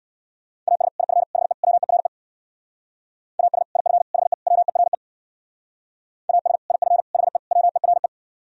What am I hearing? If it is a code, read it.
Morse code, heard as a keyed tone, about 61 wpm, 700 Hz. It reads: AS IF HE WERE AS IF HE WERE AS IF HE WERE
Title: Speed-Racing - Sets of 4 Words 50wpm